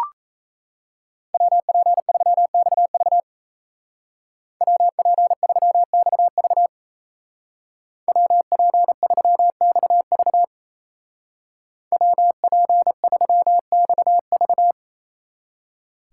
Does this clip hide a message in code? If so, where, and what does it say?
Yes: Morse code from 1.3 to 14.7 s, heard as a keyed tone, about 35 wpm, 700 Hz. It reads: WP3XV WP3XV WP3XV WP3XV